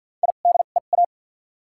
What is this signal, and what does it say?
Morse code, heard as a keyed tone, about 47 wpm, spaced as for 35 wpm, 700 Hz. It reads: IDEA
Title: 2x Character Spacing - Top 300 Words - 50wpm